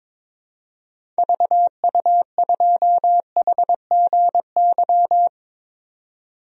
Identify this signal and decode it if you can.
Morse code, heard as a keyed tone, about 22 wpm, 700 Hz. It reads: VU2HGY
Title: DXCC-100 Call Signs - 6 Characters - Repeated 3x - 22wpm